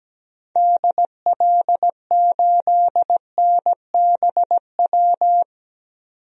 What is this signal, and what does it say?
Morse code, heard as a keyed tone, about 17 wpm, 700 Hz. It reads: DL8NBW
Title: DXCC-100 Call Signs - 6 Characters - 17wpm